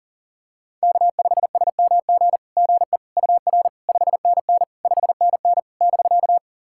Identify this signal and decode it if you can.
Morse code, heard as a keyed tone, about 40 wpm, 700 Hz. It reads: K5SMG GE UR 5NN 5NN <BK>
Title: SOTA QSO Practice 40wpm